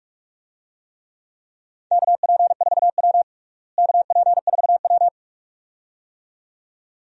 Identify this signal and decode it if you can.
Morse code, heard as a keyed tone, about 46 wpm, 700 Hz. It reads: KP4W KP4W
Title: DXCC-100 Call Signs - 4 Characters - Repeated 2x - 45wpm